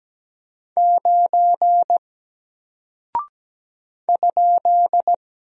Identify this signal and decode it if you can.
Morse code, heard as a keyed tone, about 17 wpm, 700 Hz. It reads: 9 ?
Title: Single Character (Letter:Number:Punctuation:Prosign) Alt 17wpm